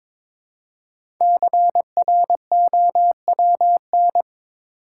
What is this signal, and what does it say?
Morse code, heard as a keyed tone, about 22 wpm, 700 Hz. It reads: CROWN
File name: 5 Letter Words - First 3 Letters Given - 22wpm